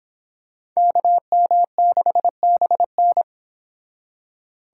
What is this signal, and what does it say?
Morse code, heard as a keyed tone, about 26 wpm, 700 Hz. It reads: KM6BN